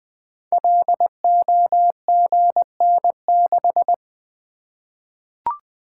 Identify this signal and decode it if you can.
Morse code, heard as a keyed tone, about 20 wpm, 700 Hz. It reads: LOGN6
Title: Code Groups - 5 Letter:Numbers - 20wpm